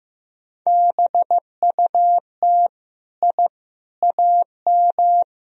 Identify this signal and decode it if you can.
Morse code, heard as a keyed tone, about 15 wpm, 700 Hz. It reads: BUT I AM
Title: Sentences from Top 1000 Words - Review Part 4 - 15wpm